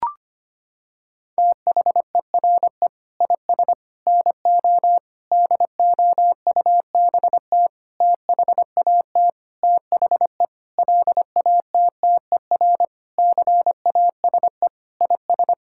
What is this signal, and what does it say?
Morse code, heard as a keyed tone, about 25 wpm, 700 Hz. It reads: THERE IS NO DOUBT THAT THE LATTER CASE IS
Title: Sentences from Top 1600 Words 25wpm